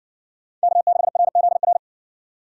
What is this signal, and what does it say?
Morse code, heard as a keyed tone, about 57 wpm, 700 Hz. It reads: K6RZR